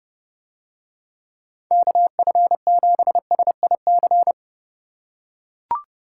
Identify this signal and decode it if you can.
Morse code, heard as a keyed tone, about 30 wpm, 700 Hz. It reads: KF7SIC